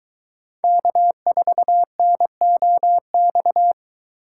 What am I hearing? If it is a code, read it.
Morse code, heard as a keyed tone, about 22 wpm, 700 Hz. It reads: K4NOX